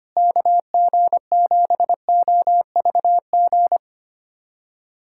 Morse code, heard as a keyed tone, about 25 wpm, 700 Hz. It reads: KG7OVG
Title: USA Callsign DE Callsign - 6 Characters - 25wpm